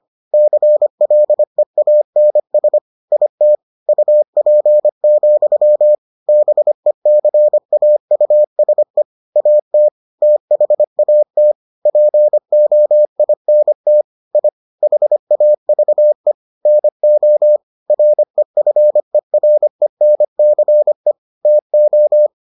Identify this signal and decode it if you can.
Morse code, heard as a keyed tone, about 25 wpm, 600 Hz. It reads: CLEANS IT UP, BECAUSE AT THAT POINT I HAVE NO REFERENCE TO